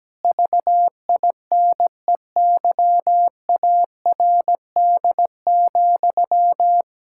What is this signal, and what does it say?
Morse code, heard as a keyed tone, about 17 wpm, 700 Hz. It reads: VINEYARD,
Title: Pangram Challenge v2 - 17wpm